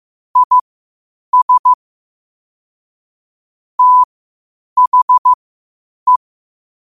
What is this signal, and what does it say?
Morse code, heard as a keyed tone, about 15 wpm, spaced as for 8 wpm, 1000 Hz. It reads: IS THE
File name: qotdfiles / 8wpm